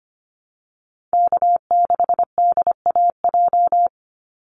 Morse code, heard as a keyed tone, about 25 wpm, 700 Hz. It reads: K6DAJ
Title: Call Signs Repeated 2x - 25wpm